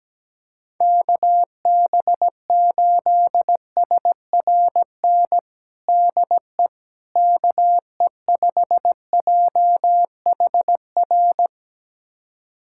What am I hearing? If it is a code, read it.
Morse code, heard as a keyed tone, about 17 wpm, 700 Hz. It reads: KB8SRN DE KE5JHR
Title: USA Callsign DE Callsign - 6 Characters - 17wpm